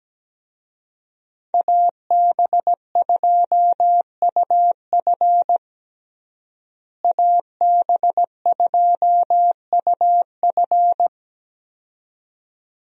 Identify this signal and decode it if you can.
Morse code, heard as a keyed tone, about 17 wpm, 700 Hz. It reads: AB2UF AB2UF
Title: USA Call Signs - 5 Characters - Repeated 2x - 17wpm